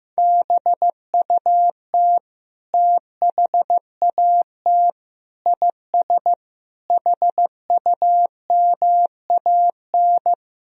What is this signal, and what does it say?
Morse code, heard as a keyed tone, about 15 wpm, 700 Hz. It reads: BUT THAT IS HUMAN
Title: Sentences from Top 1000 Words - Review Part 3 - 15wpm